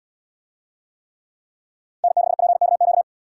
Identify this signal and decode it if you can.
Morse code, heard as a keyed tone, about 78 wpm, 700 Hz. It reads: N6PKQ